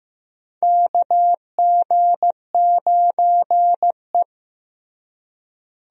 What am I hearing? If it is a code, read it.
Morse code, heard as a keyed tone, about 15 wpm, 700 Hz. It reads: KG9E
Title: USA Call Signs - 4 Characters - Repeated 2x - 15wpm